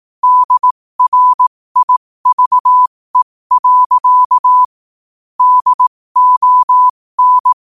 Morse code, heard as a keyed tone, about 18 wpm, spaced as for 15 wpm, 1000 Hz. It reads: DRIVE. DON
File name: 2024-11-18-15wpm